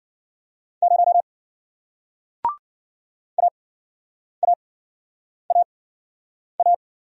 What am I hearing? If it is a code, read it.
Morse code, heard as a keyed tone, about 53 wpm, spaced as for 9 wpm, 700 Hz. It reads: 0 AAAA